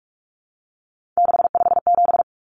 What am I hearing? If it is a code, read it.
Morse code, heard as a keyed tone, about 44 wpm, 700 Hz. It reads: 657
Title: Numbers - 3-Digit 45wpm